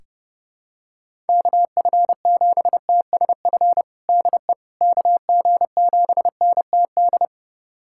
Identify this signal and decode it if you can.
Morse code, heard as a keyed tone, about 32 wpm, 700 Hz. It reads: KF7TSF DE KG7NTD
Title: USA Callsign DE Callsign - 6 Characters - 30wpm